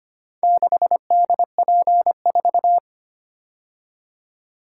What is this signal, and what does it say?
Morse code, heard as a keyed tone, about 25 wpm, 700 Hz. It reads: 6DP4
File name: practice / 25wpm / Four Letter-Number Combinations - 25wpm